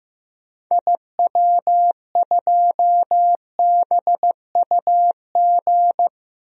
Morse code, heard as a keyed tone, about 15 wpm, 700 Hz. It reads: IW2BUG
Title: DXCC-100 Call Signs - 6 Characters - Repeated 2x - 15wpm